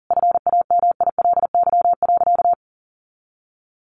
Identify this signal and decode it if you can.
Morse code, heard as a keyed tone, about 40 wpm, 700 Hz. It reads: FAMILY.
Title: Sentences from Top 1300 Words 40wpm